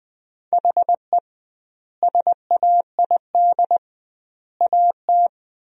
Morse code, heard as a keyed tone, about 21 wpm, 700 Hz. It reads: HE SAID AT
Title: 2x Word-Spacing - Sentences from Top 300 Words - 20wpm